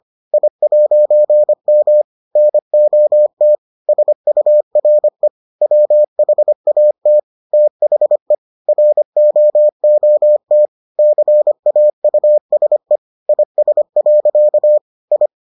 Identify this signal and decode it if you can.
Morse code, heard as a keyed tone, about 25 wpm, 600 Hz. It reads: I'M NOT SURE WHAT THE ROOT CAUSE IS. I